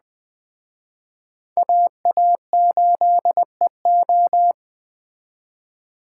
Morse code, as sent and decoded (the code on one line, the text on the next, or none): .- .- ---.. ---
AA8EO